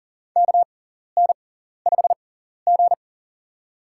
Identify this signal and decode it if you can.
Morse code, heard as a keyed tone, about 40 wpm, spaced as for 14 wpm, 700 Hz. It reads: KN5G
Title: USA Call Signs - 4 Characters - 6x Character Spacing - 40wpm